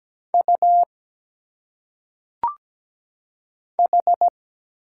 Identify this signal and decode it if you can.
Morse code, heard as a keyed tone, about 17 wpm, 700 Hz. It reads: U H
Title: Single Letters - Phonetic Pronunciation 17wpm